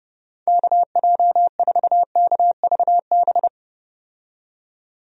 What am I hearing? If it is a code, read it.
Morse code, heard as a keyed tone, about 32 wpm, 700 Hz. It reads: KJ4KVB
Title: USA Call Signs - 6 Characters - 30wpm